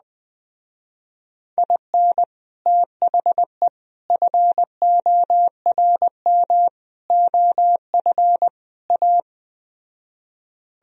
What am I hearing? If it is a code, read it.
Morse code, heard as a keyed tone, about 20 wpm, 700 Hz. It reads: IN THE FORM OF A